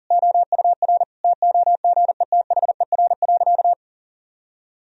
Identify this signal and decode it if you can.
Morse code, heard as a keyed tone, about 40 wpm, 700 Hz. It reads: OUR TOGETHER.
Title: Sentences from Top 1000 Words - Review Part 1 - 40wpm